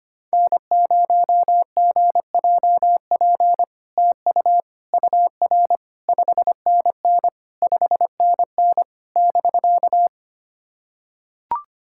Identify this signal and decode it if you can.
Morse code, heard as a keyed tone, about 25 wpm, 700 Hz. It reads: N0GJP TU UR 5NN 5NN <BK>
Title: SOTA QSO Practice 25wpm